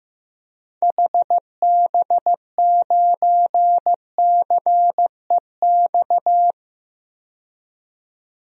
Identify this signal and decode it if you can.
Morse code, heard as a keyed tone, about 15 wpm, 700 Hz. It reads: HB9CEX
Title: DXCC-100 Call Signs - 6 Characters - 15wpm